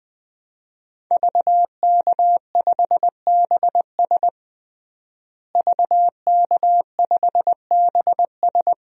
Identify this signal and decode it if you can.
Morse code, heard as a keyed tone, about 20 wpm, 700 Hz. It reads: VK5BS VK5BS
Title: DXCC-100 Call Signs - 5 Characters - Repeated 2x - 20wpm